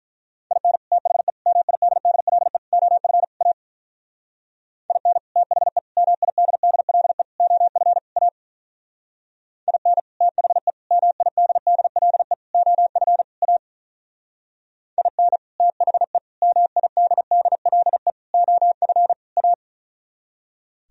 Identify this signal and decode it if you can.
Morse code, heard as a keyed tone, about 46 wpm, 700 Hz. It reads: IN THE MIDDLE OF A IN THE MIDDLE OF A IN THE MIDDLE OF A IN THE MIDDLE OF A